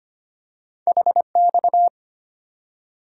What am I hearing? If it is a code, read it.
Morse code, heard as a keyed tone, about 25 wpm, 700 Hz. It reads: HX
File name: Two Letter-Number Combinations - 25wpm